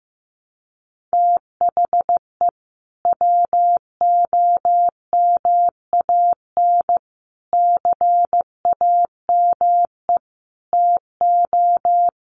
Morse code, heard as a keyed tone, about 15 wpm, 700 Hz. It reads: THE WOMAN CAME TO
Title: Sentences from Top 1100 Words 15wpm